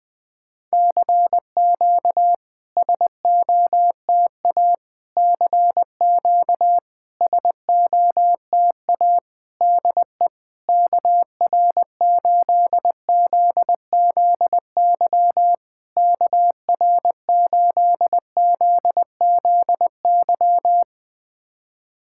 Morse code, heard as a keyed tone, about 21 wpm, 700 Hz. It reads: CQ SOTA CQ SOTA DE KR8ZZY KR8ZZY
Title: SOTA QSO Practice 20wpm